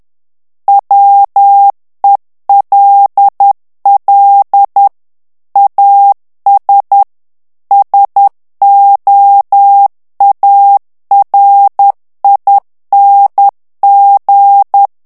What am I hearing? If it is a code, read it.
Morse code, heard as a keyed tone, about 10 wpm, 800 Hz. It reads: WELL AS SOARING
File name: ANGLAIS texte clair 2 vitesse 10 mots minute